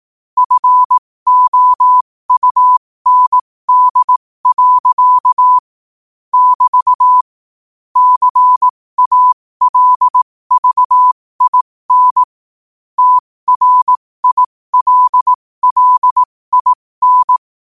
Morse code, heard as a keyed tone, about 18 wpm, spaced as for 15 wpm, 1000 Hz. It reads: FOUND. <BT> CALVIN TRILLIN
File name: qotdfiles / 15wpm